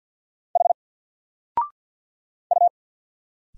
Morse code, heard as a keyed tone, about 50 wpm, 700 Hz. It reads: H U
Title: Single Letter-Number 50wpm